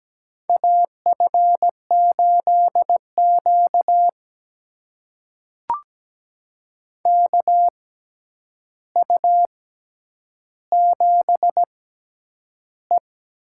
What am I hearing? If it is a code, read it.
Morse code, heard as a keyed tone, about 17 wpm, spaced as for 9 wpm, 700 Hz. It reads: AF8Q KU7E
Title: USA Call Signs - 4 Characters - 6x Character Spacing - 17wpm